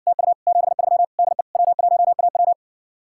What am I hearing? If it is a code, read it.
Morse code, heard as a keyed tone, about 59 wpm, 700 Hz. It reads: TU 73 DE W1AW